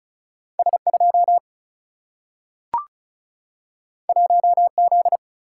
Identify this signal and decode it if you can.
Morse code, heard as a keyed tone, about 36 wpm, 700 Hz. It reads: S2 1Z
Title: Single-Letter Single-Number 35wpm